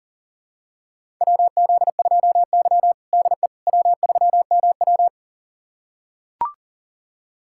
Morse code, heard as a keyed tone, about 40 wpm, 700 Hz. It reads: WZ2Y DE W3MW
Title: USA Callsign DE Callsign - 4 Characters - 40wpm